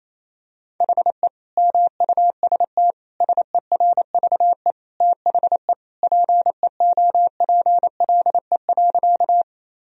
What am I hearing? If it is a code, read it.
Morse code, heard as a keyed tone, about 28 wpm, 700 Hz. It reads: HE MUST SERVE THE PEOPLE.
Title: Sentences from Top 1000 Words - Review Part 2 - 28wpm